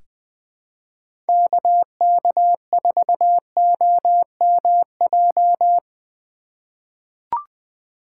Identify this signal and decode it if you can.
Morse code, heard as a keyed tone, about 20 wpm, 700 Hz. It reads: KK4OMJ
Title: 2023 POTA Hunter Call Signs - 20wpm